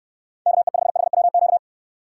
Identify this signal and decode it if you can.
Morse code, heard as a keyed tone, about 65 wpm, 700 Hz. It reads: KE5HWQ